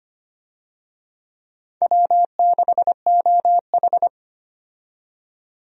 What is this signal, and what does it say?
Morse code, heard as a keyed tone, about 25 wpm, 700 Hz. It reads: W6OH
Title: USA Call Signs - 4 Characters - 25wpm